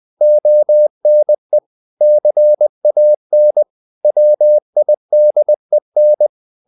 Morse code, heard as a keyed tone, about 20 wpm, 600 Hz. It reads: ONE CAN WIDEN
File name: the old man and the sea part 2 20wpm